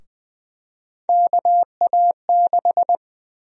Morse code, heard as a keyed tone, about 20 wpm, 700 Hz. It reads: KA6
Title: USA Call Sign Prefixes - 20wpm